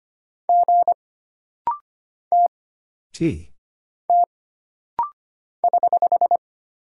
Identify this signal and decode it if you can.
Morse code, heard as a keyed tone, about 25 wpm, 700 Hz. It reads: G T T <HH>
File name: Single Character (Letter:Number:Punctuation:Prosign) - Rapid-Fire - 25wpm